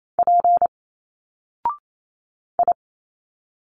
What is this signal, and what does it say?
Morse code, heard as a keyed tone, about 28 wpm, 700 Hz. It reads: P I
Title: Single Letters 28wpm